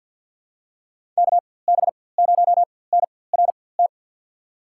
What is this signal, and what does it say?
Morse code, heard as a keyed tone, about 52 wpm, spaced as for 24 wpm, 700 Hz. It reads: KB0NRT
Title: USA Call Signs - 6 Characters - 4x Character Spacing - 50wpm